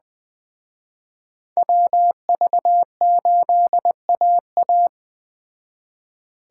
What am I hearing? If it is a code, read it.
Morse code, heard as a keyed tone, about 20 wpm, 700 Hz. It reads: WV8AA